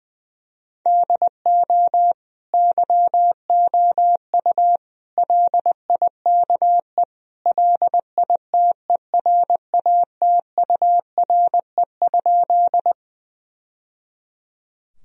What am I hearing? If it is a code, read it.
Morse code, heard as a keyed tone, about 21 wpm, 700 Hz. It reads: DO YOU LIKE LITERATURE?
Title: Sentences from Top 1000 Words - Review Part 3 - 20wpm